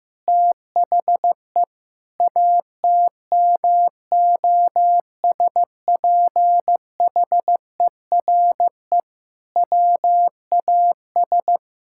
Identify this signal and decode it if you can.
Morse code, heard as a keyed tone, about 15 wpm, 700 Hz. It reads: THE ATMOSPHERE WAS
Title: Sentences from Top 1600 Words 15wpm